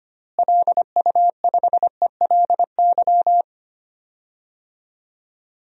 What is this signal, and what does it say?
Morse code, heard as a keyed tone, about 25 wpm, 700 Hz. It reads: LU5ELY
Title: DXCC-100 Call Signs - 6 Characters - Repeated 2x - 25wpm